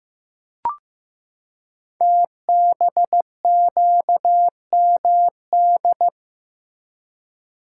Morse code, heard as a keyed tone, about 15 wpm, 700 Hz. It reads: TBQMD
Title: Code Groups - 5 Letters - 15wpm